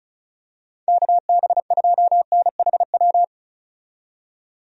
Morse code, heard as a keyed tone, about 36 wpm, 700 Hz. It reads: KB2NHW
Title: USA Call Signs - 6 Characters - 35wpm